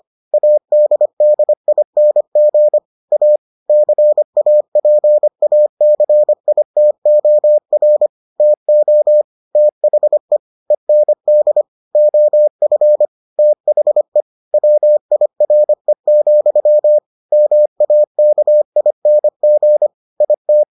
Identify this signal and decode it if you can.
Morse code, heard as a keyed tone, about 25 wpm, 600 Hz. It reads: ADDING A CAPACITOR TO THE END OF THE WIRE, MAKING IT